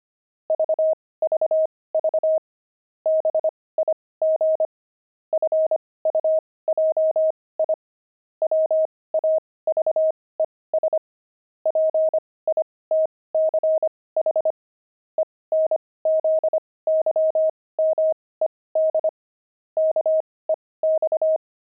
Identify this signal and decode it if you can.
Morse code, heard as a keyed tone, about 25 wpm, spaced as for 18 wpm, 630 Hz. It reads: VVV BIG FUJI WAVES PITCH ENZYMED KEX